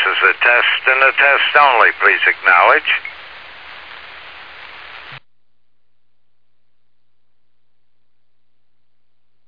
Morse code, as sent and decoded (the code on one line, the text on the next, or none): none